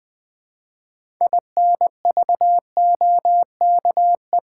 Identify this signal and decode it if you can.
Morse code, heard as a keyed tone, about 20 wpm, 700 Hz. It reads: INVOKE
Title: 6 Letter Words - First 3 Letters Given - 20wpm